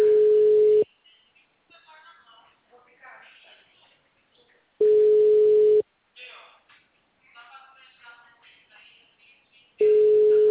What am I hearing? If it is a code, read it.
Ringback tone (Europe, 425 Hz), with speech.
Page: Atibaia - SP